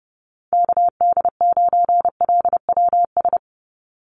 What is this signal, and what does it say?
Morse code, heard as a keyed tone, about 30 wpm, 700 Hz. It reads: KD9LWS
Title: USA Call Signs 30wpm